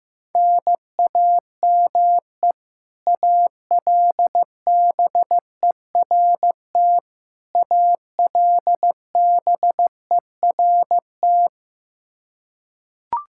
Morse code, heard as a keyed tone, about 15 wpm, 700 Hz. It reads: NAME ALBERT ALBERT
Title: Practice QSO - Exchange 1 - Beginning - 15wpm